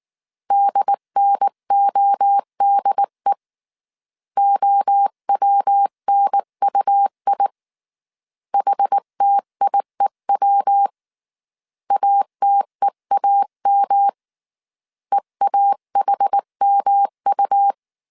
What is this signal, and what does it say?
Morse code, heard as a keyed tone, about 19 wpm, 800 Hz. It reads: DNODE OWNUI HTIEW ATEAM EAHMU
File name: CW / DET / 1 3 1500C S B 18wpm 1916